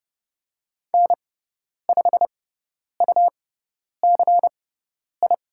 Morse code, heard as a keyed tone, about 30 wpm, spaced as for 10 wpm, 690 Hz. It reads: N5UCI
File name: Lesson 18g - Letter m - Call Signs 30wpm